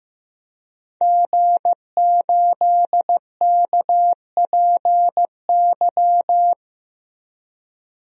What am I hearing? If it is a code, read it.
Morse code, heard as a keyed tone, about 15 wpm, 700 Hz. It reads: G8KPY
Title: DXCC-100 Call Signs - 5 Characters - Repeated 2x - 15wpm